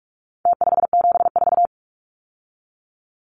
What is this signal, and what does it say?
Morse code, heard as a keyed tone, about 44 wpm, 700 Hz. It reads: T574
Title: Numbers - Contest Style - 4-Digit 45wpm